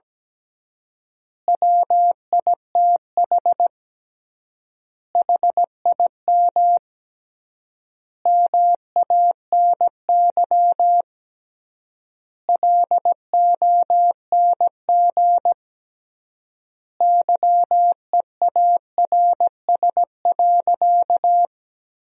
Morse code, heard as a keyed tone, about 17 wpm, 700 Hz. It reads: WITH HIM MANY LONG YEARS.
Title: 3x Word-Spacing - Sentences from Top 200 Words - 17wpm